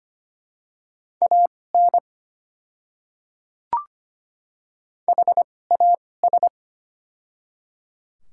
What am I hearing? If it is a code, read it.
Morse code, heard as a keyed tone, about 25 wpm, spaced as for 18 wpm, 690 Hz. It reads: AN HAS
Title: 2x Character Spacing - Top 100 Words - 25wpm